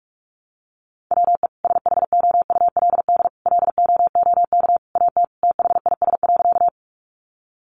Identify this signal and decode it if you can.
Morse code, heard as a keyed tone, about 44 wpm, 700 Hz. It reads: WE SHOULD LOOK AT THIS.
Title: Sentences from Top 300 Words 45wpm